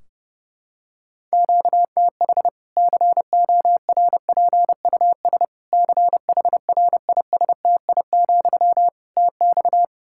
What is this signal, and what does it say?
Morse code, heard as a keyed tone, about 30 wpm, 700 Hz. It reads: QTH CORPUS CHRISTI, TX
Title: K1USN SST QSO Practice 30wpm